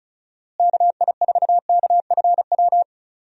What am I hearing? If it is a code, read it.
Morse code, heard as a keyed tone, about 36 wpm, 700 Hz. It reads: KI4KFW
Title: USA Call Signs - 6 Characters - Repeated 3x - 35wpm